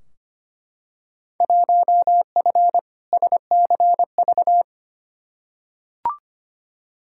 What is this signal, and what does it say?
Morse code, heard as a keyed tone, about 25 wpm, 700 Hz. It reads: 1F SCV
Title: Field Day Exchanges Encore - 25wpm